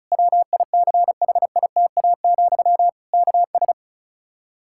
Morse code, heard as a keyed tone, about 36 wpm, 700 Hz. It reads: WICHITA, KS
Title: Practice QSO - Exchange 1 - Beginning - 35wpm